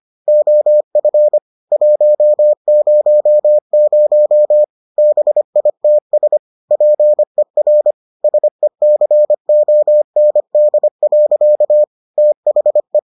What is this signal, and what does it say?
Morse code, heard as a keyed tone, about 25 wpm, 600 Hz. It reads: OF 100 BITS PER SECOND. THE